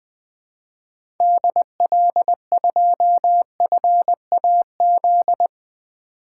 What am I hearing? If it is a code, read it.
Morse code, heard as a keyed tone, about 20 wpm, 700 Hz. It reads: DL2FAZ
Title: 2023 SOTA Activator Call Signs - 20wpm